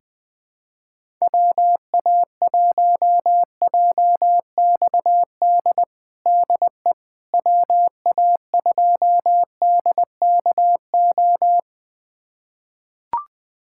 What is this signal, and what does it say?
Morse code, heard as a keyed tone, about 21 wpm, 700 Hz. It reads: WA1JXD DE WA2DKO
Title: USA Callsign DE Callsign - 6 Characters - 20wpm